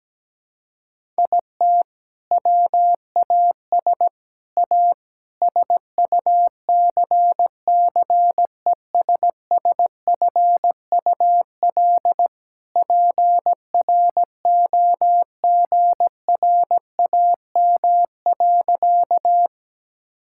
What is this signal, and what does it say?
Morse code, heard as a keyed tone, about 17 wpm, 700 Hz. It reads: IT WAS A SUCCESSFUL PROGRAM.
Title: Sentences from Top 1000 Words - Review Part 5 - 17wpm